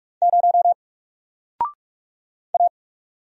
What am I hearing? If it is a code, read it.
Morse code, heard as a keyed tone, about 46 wpm, 700 Hz. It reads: 0 A
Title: Single Character (Letter:Number:Punctuation:Prosign) 45wpm